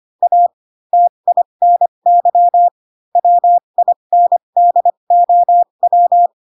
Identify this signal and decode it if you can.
Morse code, heard as a keyed tone, about 25 wpm, spaced as for 22 wpm, 700 Hz. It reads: A TINY WINDOW
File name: Pinocchio / Pinocchio ch3 18 25 ik1hge